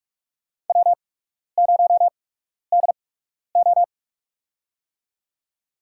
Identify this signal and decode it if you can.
Morse code, heard as a keyed tone, about 44 wpm, spaced as for 12 wpm, 690 Hz. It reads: W0DO
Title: USA Call Signs - 8x Character Spacing - 45wpm